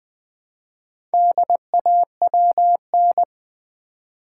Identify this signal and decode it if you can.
Morse code, heard as a keyed tone, about 20 wpm, 700 Hz. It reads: DAWN
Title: Build A Word - 1 Letter At A Time - 4-Letter Words - 20wpm